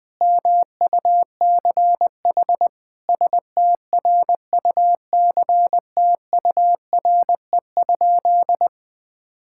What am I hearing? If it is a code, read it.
Morse code, heard as a keyed tone, about 20 wpm, 700 Hz. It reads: MUCH STRUCTURE?
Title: Sentences from Top 1000 Words - Review Part 6 - 20wpm